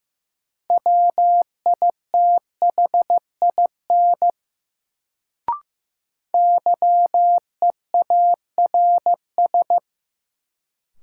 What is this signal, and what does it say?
Morse code, heard as a keyed tone, about 15 wpm, 700 Hz. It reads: WITHIN YEARS
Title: Sets of 2 Words - New Word Drill - 15wpm